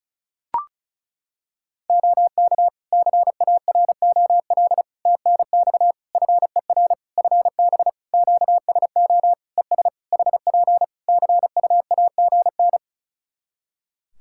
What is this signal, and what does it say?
Morse code, heard as a keyed tone, about 36 wpm, 700 Hz. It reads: OK CAROL TNX FER FB QSO ES HP CUAGN
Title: Practice QSO - Exchange 4 - Ending - 35wpm